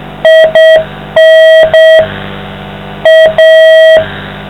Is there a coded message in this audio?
Yes: Morse code from 0.3 to 4.0 s, heard as a keyed tone, about 7 wpm, 640 Hz, with noise. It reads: IN A